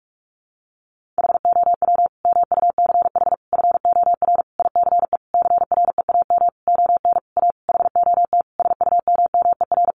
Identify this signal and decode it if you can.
Morse code, heard as a keyed tone, about 44 wpm, 700 Hz. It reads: HOW MUCH FOR ICE CREAM ON A HOT SUMMER
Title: Sentences from Top 1500 Words 45wpm